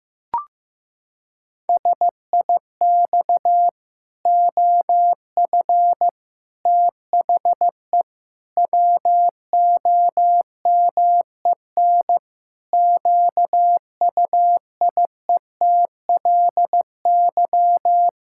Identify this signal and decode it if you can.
Morse code, heard as a keyed tone, about 15 wpm, 700 Hz. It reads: SIX OF THE WOMEN QUIETLY
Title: Pangram Challenge v2 - 15wpm